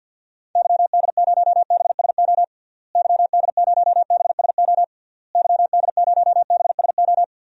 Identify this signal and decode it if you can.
Morse code, heard as a keyed tone, about 49 wpm, 700 Hz. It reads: YD0BSO YD0BSO YD0BSO
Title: DXCC-100 Call Signs - 6 Characters - Repeated 3x - 50wpm